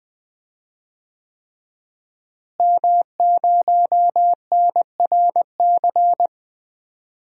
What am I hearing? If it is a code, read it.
Morse code, heard as a keyed tone, about 21 wpm, 700 Hz. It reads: M0NRC
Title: DXCC-100 Call Signs - 5 Characters - Repeated 3x - 20wpm